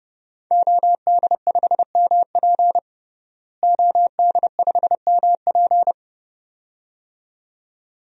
Morse code, heard as a keyed tone, about 30 wpm, 700 Hz. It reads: OD5MP OD5MP
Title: DXCC-100 Call Signs - 5 Characters - Repeated 2x - 30wpm